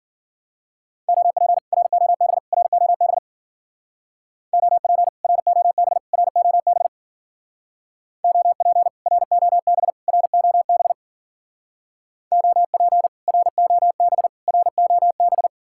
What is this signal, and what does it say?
Morse code, heard as a keyed tone, about 49 wpm, 700 Hz. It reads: OP ROB ROB OP ROB ROB OP ROB ROB OP ROB ROB
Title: Speed-Racing Common QSO Phrases - 40wpm